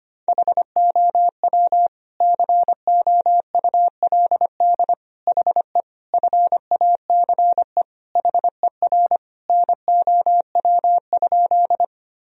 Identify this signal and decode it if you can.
Morse code, heard as a keyed tone, about 25 wpm, 700 Hz. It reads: HOW COULD HE FACE HER NOW?
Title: Sentences from Top 1000 Words - Review Part 2 - 25wpm